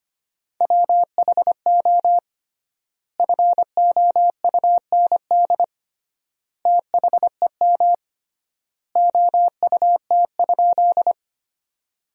Morse code, heard as a keyed tone, about 25 wpm, 700 Hz. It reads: WHO FOUND THEM OUT?
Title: 3x Word-Spacing - Sentences from Top 300 Words - 25wpm